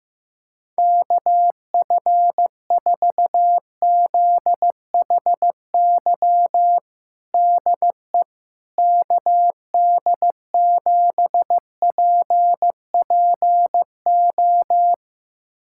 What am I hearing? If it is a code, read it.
Morse code, heard as a keyed tone, about 15 wpm, 700 Hz. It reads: KF4ZHY DE KD7PPO
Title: USA Callsign DE Callsign 15wpm